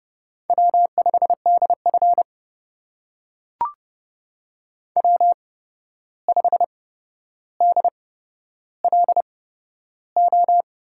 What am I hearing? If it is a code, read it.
Morse code, heard as a keyed tone, about 30 wpm, spaced as for 12 wpm, 700 Hz. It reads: W5DF W5DLO
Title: USA Call Signs - 8x Character Spacing - 30wpm